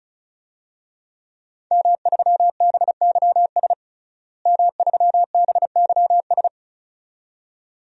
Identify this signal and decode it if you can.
Morse code, heard as a keyed tone, about 34 wpm, 700 Hz. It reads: M3BYS M3BYS
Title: DXCC-100 Call Signs - 5 Characters - Repeated 2x - 35wpm